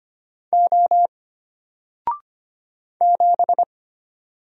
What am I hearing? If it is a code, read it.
Morse code, heard as a keyed tone, about 25 wpm, 700 Hz. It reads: O 7
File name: Single-Letter-Number-25wpm